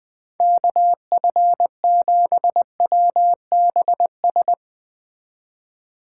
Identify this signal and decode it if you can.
Morse code, heard as a keyed tone, about 20 wpm, 700 Hz. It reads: KF7WBS